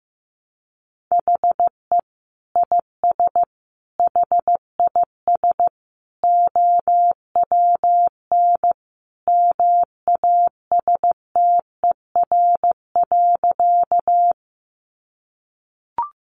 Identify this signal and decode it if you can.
Morse code, heard as a keyed tone, about 15 wpm, 700 Hz. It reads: HE IS HIS OWN MASTER.
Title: Sentences from Top 1300 Words 15wpm